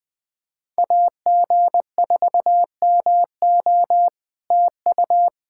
Morse code, heard as a keyed tone, about 20 wpm, 700 Hz. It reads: AG4MO TU
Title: SOTA QSO Practice 20wpm